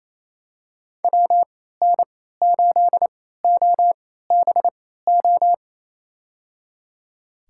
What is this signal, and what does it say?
Morse code, heard as a keyed tone, about 28 wpm, spaced as for 16 wpm, 700 Hz. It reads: WN8OBO
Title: USA Call Signs - 6 Characters - 3x Character Spacing - 28wpm